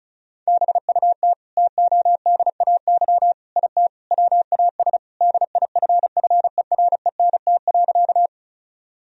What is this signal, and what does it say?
Morse code, heard as a keyed tone, about 36 wpm, 700 Hz. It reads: BUT TODAY IT WAS DIFFERENT.
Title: Sentences from Top 1000 Words - Review Part 4 - 35wpm